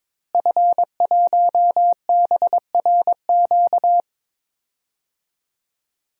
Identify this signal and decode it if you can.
Morse code, heard as a keyed tone, about 21 wpm, 700 Hz. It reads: F1BRQ
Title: DXCC-100 Call Signs - 5 Characters - Repeated 3x - 22wpm